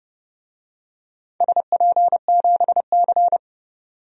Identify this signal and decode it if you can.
Morse code, heard as a keyed tone, about 30 wpm, 700 Hz. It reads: SP7C
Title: DXCC-100 Call Signs - 4 Characters - 30wpm